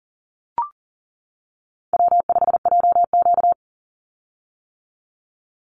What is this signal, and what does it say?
Morse code, heard as a keyed tone, about 39 wpm, 700 Hz. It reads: W5JQ
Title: Call Signs - 40wpm